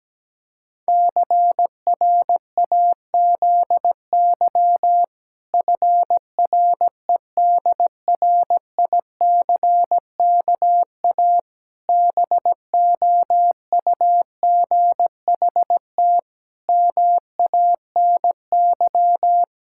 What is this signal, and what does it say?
Morse code, heard as a keyed tone, about 17 wpm, 700 Hz. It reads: CRAZY FREDRICKA BOUGHT MANY
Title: Pangram Challenge v2 - 17wpm